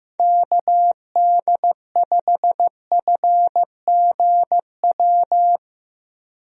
Morse code, heard as a keyed tone, about 15 wpm, 700 Hz. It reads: KD5FGW
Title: USA Call Signs - 6 Characters - 15wpm